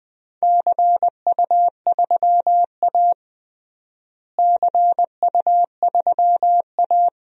Morse code, heard as a keyed tone, about 21 wpm, 700 Hz. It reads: CU3A CU3A
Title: DXCC-100 Call Signs - 4 Characters - Repeated 2x - 20wpm